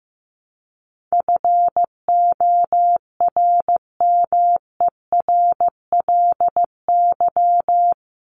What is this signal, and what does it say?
Morse code, heard as a keyed tone, about 14 wpm, 700 Hz. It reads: FORMERLY
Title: LY Suffix - Top 100 Words with Base Words 15wpm